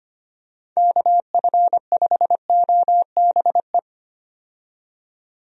Morse code, heard as a keyed tone, about 25 wpm, 700 Hz. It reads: KF5OBE
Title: USA Call Signs - 6 Characters - 25wpm